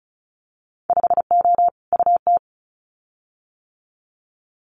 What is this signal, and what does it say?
Morse code, heard as a keyed tone, about 36 wpm, 700 Hz. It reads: 5O UT
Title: Winter Field Day Exchanges 35wpm